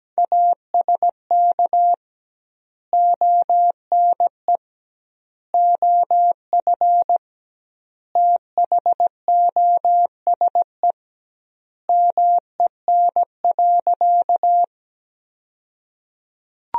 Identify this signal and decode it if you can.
Morse code, heard as a keyed tone, about 17 wpm, 700 Hz. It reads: ASK ONE OF THOSE MEN.
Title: 2x Word-Spacing - Sentences from Top 300 Words - 17wpm